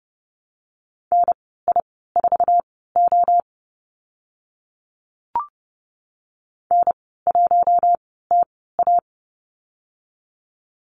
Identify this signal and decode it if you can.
Morse code, heard as a keyed tone, about 30 wpm, spaced as for 17 wpm, 700 Hz. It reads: NI4O N1TA
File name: Lesson 15g - Number 5 - Call Signs 30wpm